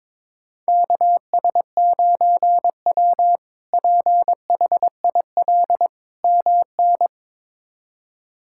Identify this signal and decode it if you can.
Morse code, heard as a keyed tone, about 21 wpm, 700 Hz. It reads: KS9W PHIL MN
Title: K1USN SST QSO Practice - 22wpm